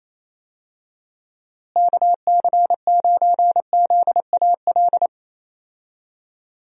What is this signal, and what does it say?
Morse code, heard as a keyed tone, about 28 wpm, 700 Hz. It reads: KC9ZAL
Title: USA Call Signs - 6 Characters - Repeated 2x - 28wpm